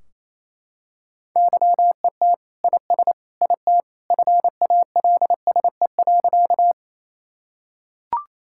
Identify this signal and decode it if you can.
Morse code, heard as a keyed tone, about 28 wpm, 700 Hz. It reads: YET IS IT FALSE.
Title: Sentences from Top 1600 Words 28wpm